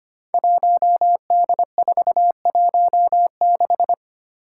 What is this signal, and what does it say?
Morse code, heard as a keyed tone, about 25 wpm, 700 Hz. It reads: 1D416
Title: Five Challenging Letter-Number Combinations 25wpm